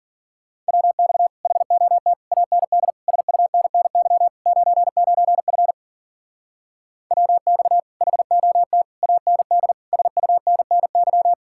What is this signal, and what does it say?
Morse code, heard as a keyed tone, about 49 wpm, 700 Hz. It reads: WX HOT AND SUNNY 99F WX HOT AND SUNNY
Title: Speed-Racing Common QSO Phrases - 40wpm